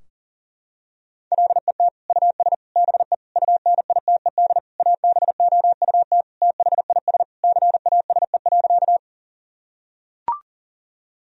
Morse code, heard as a keyed tone, about 40 wpm, 700 Hz. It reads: LET US BE UNITED ABOUT THIS CASE.
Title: Sentences from Top 1000 Words - Review Part 4 - 40wpm